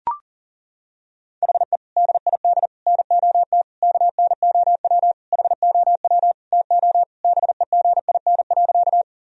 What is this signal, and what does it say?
Morse code, heard as a keyed tone, about 37 wpm, 700 Hz. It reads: HE DID NOT KNOW HOW TO BEGIN.
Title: Sentences from Top 1000 Words - Review Part 1 - 40wpm